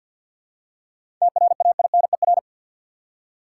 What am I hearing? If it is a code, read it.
Morse code, heard as a keyed tone, about 52 wpm, 700 Hz. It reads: TRAINER
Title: Seven Letter Words - 50wpm